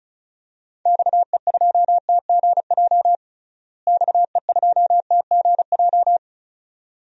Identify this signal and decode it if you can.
Morse code, heard as a keyed tone, about 36 wpm, 700 Hz. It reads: XE2TGJ XE2TGJ
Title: DXCC-100 Call Signs - 6 Characters - Repeated 2x - 35wpm